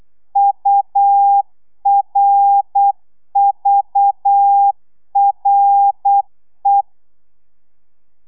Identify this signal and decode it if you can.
Morse code, heard as a keyed tone, about 8 wpm, 800 Hz. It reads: URVRE
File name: telegrafia / i2tfj / Lez6 35